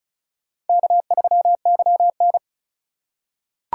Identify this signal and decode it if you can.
Morse code, heard as a keyed tone, about 34 wpm, 700 Hz. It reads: K3YN
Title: USA Call Signs - 4 Characters - 2x Character Spacing - 35wpm